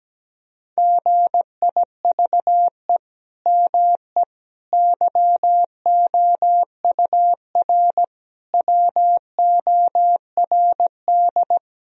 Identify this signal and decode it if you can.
Morse code, heard as a keyed tone, about 17 wpm, 700 Hz. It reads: GIVE ME YOUR WORD
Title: Sentences from Top 1000 Words - Review Part 3 - 17wpm